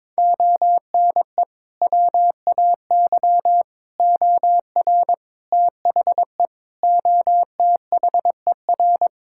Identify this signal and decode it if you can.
Morse code, heard as a keyed tone, about 22 wpm, 700 Hz. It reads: ONE WAY OR THE OTHER
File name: practice / 22wpm / Build A Phrase - 5-Word Phrases - 22wpm